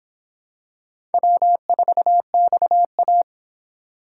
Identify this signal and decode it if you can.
Morse code, heard as a keyed tone, about 26 wpm, 700 Hz. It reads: W4XA